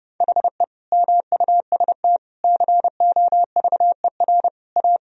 Morse code, heard as a keyed tone, about 30 wpm, 700 Hz. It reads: HE MUST COVER A
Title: Sentences from Top 500 Words and Abbreviations 30wpm